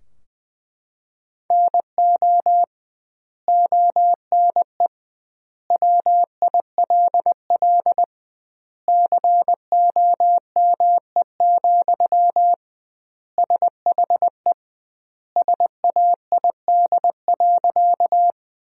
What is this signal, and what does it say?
Morse code, heard as a keyed tone, about 20 wpm, 700 Hz. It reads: NO ONE WILL COME, SHE SAID.
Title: 2x Word-Spacing - Sentences from Top 100 Words - 20wpm